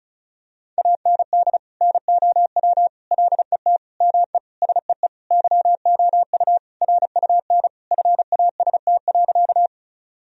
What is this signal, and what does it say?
Morse code, heard as a keyed tone, about 36 wpm, 700 Hz. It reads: AND NOW LET ME SEE YOU RUN FAST.
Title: Sentences from Top 1000 Words - Review Part 2 - 35wpm